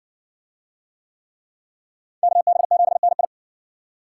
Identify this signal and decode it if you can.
Morse code, heard as a keyed tone, about 60 wpm, 700 Hz. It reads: KB7NI